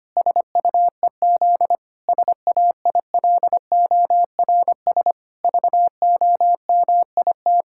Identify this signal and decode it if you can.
Morse code, heard as a keyed tone, about 25 wpm, 700 Hz. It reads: SUEZ SAILORS VOMIT
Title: Pangram Challenge v2 - 25wpm